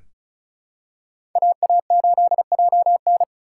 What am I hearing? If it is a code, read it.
Morse code, heard as a keyed tone, about 36 wpm, 700 Hz. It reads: AA8JN
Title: USA Call Signs - 5 Characters - 2x Character Spacing - 35wpm